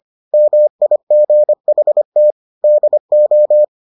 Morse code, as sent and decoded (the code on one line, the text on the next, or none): -- .. --. .... - / -.. ---
MIGHT DO